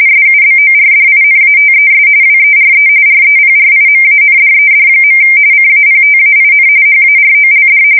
RTTY, about 45.45 baud. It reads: SUBORDINATES WERE REPRIMANDED. CAPTAIN HAROLD
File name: fsk170 / USS-Missouri